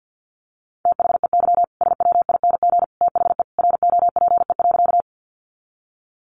Morse code, heard as a keyed tone, about 50 wpm, 700 Hz. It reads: THEY SWING THE ROPE.
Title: Easy Phrases 50wpm